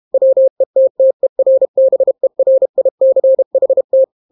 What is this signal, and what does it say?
Morse code, heard as a keyed tone, about 32 wpm, 530 Hz. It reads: WETTERBERICHT